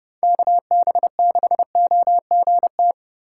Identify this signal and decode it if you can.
Morse code, heard as a keyed tone, about 30 wpm, 700 Hz. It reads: KB6OGT